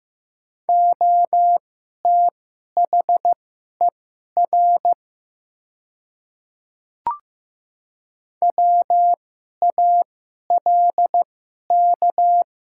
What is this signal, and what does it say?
Morse code, heard as a keyed tone, about 15 wpm, spaced as for 11 wpm, 690 Hz. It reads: OTHER WALK
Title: 2x Character Spacing - Top 500 Words - 15wpm